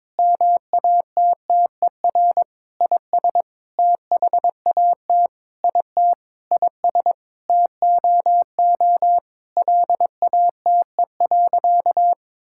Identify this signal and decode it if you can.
Morse code, heard as a keyed tone, about 22 wpm, 700 Hz. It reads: MATTER IS THAT IT IS TOO LATE.
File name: Sentences with 7-Word Phrases - 22wpm